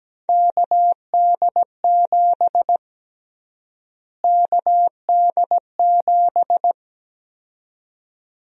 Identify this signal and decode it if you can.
Morse code, heard as a keyed tone, about 17 wpm, 700 Hz. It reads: KD7 KD7
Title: USA Call Sign Prefixes - Repeated 2x - 17wpm